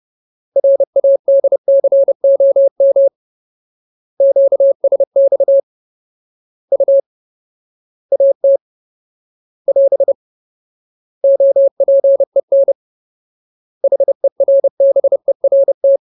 Morse code, heard as a keyed tone, about 30 wpm, 550 Hz. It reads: RADCOM QSX U AT <AS> OPEN HERBERT
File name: Ultimate Word List / 30WPM Ultimate